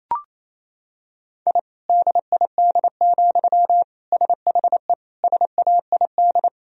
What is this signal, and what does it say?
Morse code, heard as a keyed tone, about 28 wpm, 700 Hz. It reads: I DID, SHE SAID
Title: Sentences from Top 1000 Words - Review Part 4 - 28wpm